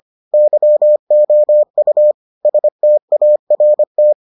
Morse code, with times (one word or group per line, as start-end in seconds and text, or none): 0.3-2.1 YOU
2.5-4.1 START